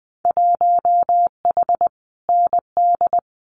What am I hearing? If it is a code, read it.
Morse code, heard as a keyed tone, about 20 wpm, 700 Hz. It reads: 1H ND
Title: Winter Field Day Exchanges 20wpm